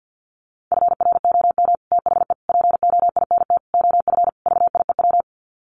Morse code, heard as a keyed tone, about 50 wpm, 700 Hz. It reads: FROM THE POINT OF VIEW
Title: Sets of 5 Words 50wpm